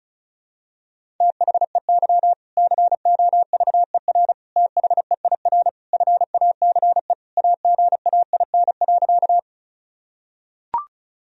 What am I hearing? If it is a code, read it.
Morse code, heard as a keyed tone, about 36 wpm, 700 Hz. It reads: THEY COVER THEIR FACE AGAIN.
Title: Sentences from Top 1000 Words - Review Part 1 - 35wpm